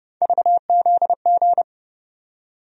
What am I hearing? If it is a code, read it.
Morse code, heard as a keyed tone, about 30 wpm, 700 Hz. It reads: VZG
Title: USA Call Sign Postfixes - 3 Letters 30wpm